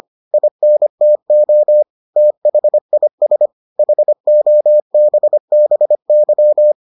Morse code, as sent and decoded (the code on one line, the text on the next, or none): .. -. - --- / - .... .. ... / .... --- -... -... -.--
INTO THIS HOBBY